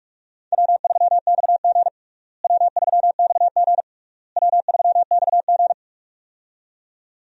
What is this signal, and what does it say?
Morse code, heard as a keyed tone, about 46 wpm, 700 Hz. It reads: W3XG W3XG W3XG
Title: USA Call Signs - 4 Characters - Repeated 3x - 45wpm